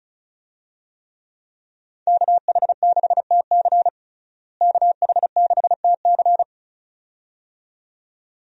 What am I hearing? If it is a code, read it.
Morse code, heard as a keyed tone, about 36 wpm, 700 Hz. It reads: KH6TC KH6TC
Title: DXCC-100 Call Signs - 5 Characters - Repeated 2x - 35wpm